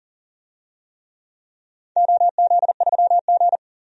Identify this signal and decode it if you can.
Morse code, heard as a keyed tone, about 39 wpm, 700 Hz. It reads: OZ3G
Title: DXCC-100 Call Signs - 4 Characters - 40wpm